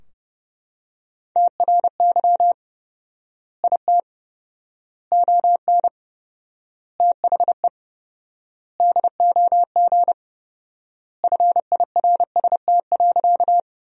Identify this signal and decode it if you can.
Morse code, heard as a keyed tone, about 30 wpm, 700 Hz. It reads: TRY IT ON THE DOG FIRST.
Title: 4x Word-Spacing - Sentences from Top 400 Words - 30wpm